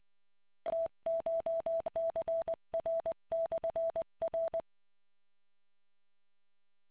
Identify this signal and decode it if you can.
Morse code, heard as a keyed tone, about 20 wpm, 660 Hz. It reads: TCHCR/R